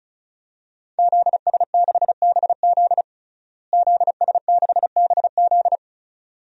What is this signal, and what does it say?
Morse code, heard as a keyed tone, about 36 wpm, 700 Hz. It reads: ZS6BZ ZS6BZ
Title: DXCC-100 Call Signs - 5 Characters - Repeated 2x - 35wpm